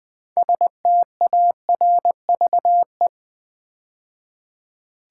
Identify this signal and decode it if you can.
Morse code, heard as a keyed tone, about 20 wpm, 700 Hz. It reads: STARVE
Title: 6 Letter Words - First 3 Letters Given - 20wpm